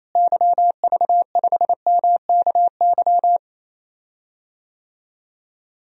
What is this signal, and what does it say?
Morse code, heard as a keyed tone, about 28 wpm, 700 Hz. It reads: YV5MKY
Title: DXCC-100 Call Signs - 6 Characters - Repeated 3x - 28wpm